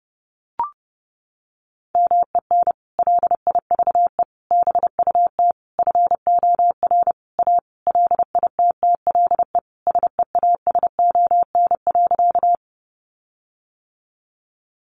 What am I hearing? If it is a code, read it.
Morse code, heard as a keyed tone, about 30 wpm, 700 Hz. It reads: MEN LIVE BUT FOR A LITTLE SEASON.
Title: Sentences from Top 1400 Words 30wpm